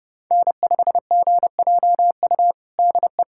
Morse code, heard as a keyed tone, about 30 wpm, 700 Hz. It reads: N5GJU DE
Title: USA Callsign DE Callsign - 5 Characters - 30wpm